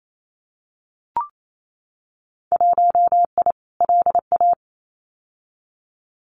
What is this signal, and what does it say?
Morse code, heard as a keyed tone, about 28 wpm, 700 Hz. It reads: 1I LA
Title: Winter Field Day Exchanges Encore 28wpm